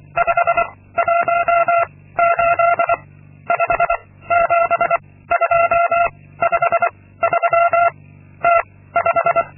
Morse code, heard as a keyed tone, about 23 wpm, 680 Hz. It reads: 5 1 8 5 7 2 5 3 T 5
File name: Linksmakalnis-2